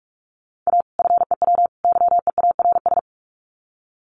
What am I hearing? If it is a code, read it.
Morse code, heard as a keyed tone, about 44 wpm, 700 Hz. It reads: A FEW YEARS
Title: Sets of 3 Words 45wpm